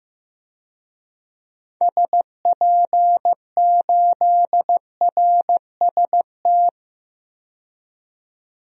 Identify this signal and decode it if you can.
Morse code, heard as a keyed tone, about 16 wpm, 700 Hz. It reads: SP8RST
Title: DXCC-100 Call Signs - 6 Characters - 15wpm